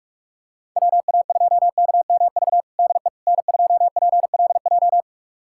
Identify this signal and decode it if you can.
Morse code, heard as a keyed tone, about 42 wpm, 700 Hz. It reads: WA2KMV DE N2PLJ
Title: USA Callsign DE Callsign 45wpm